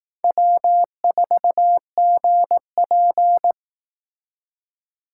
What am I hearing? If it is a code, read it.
Morse code, heard as a keyed tone, about 18 wpm, 700 Hz. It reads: W4GP
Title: Speed Racing - USA 1x2 Call Signs 15wpm